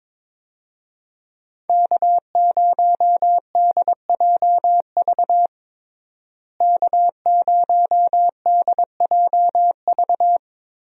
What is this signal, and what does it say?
Morse code, heard as a keyed tone, about 22 wpm, 700 Hz. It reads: K0DJV K0DJV
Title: USA Call Signs - 5 Characters - Repeated 2x - 22wpm